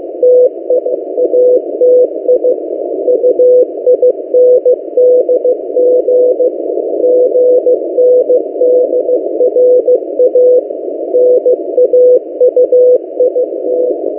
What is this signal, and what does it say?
Morse code, heard as a keyed tone, about 16 wpm, 500 Hz. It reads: TE ATI UINDG GNDRA NAUIT
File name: M08a- -1355utc-8096khz